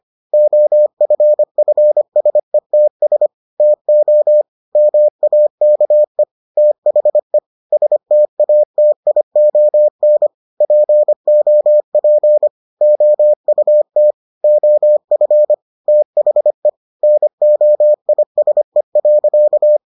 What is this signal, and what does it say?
Morse code, heard as a keyed tone, about 25 wpm, 600 Hz. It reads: OFFSETS TO MAKE THE STATION POP OUT OF THE NOISE.